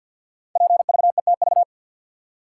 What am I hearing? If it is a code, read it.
Morse code, heard as a keyed tone, about 46 wpm, 700 Hz. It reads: PVETV
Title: Code Groups - 5 Letters - 50wpm